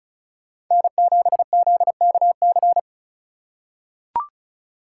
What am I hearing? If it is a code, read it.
Morse code, heard as a keyed tone, about 36 wpm, 700 Hz. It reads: N7ZKC
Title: USA Call Signs - 3x Character Spacing - 35wpm